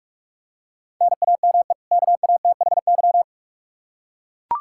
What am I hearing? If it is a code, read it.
Morse code, heard as a keyed tone, about 41 wpm, 700 Hz. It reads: NAME KATHY
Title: Practice QSO - Exchange 1 - Beginning - 45wpm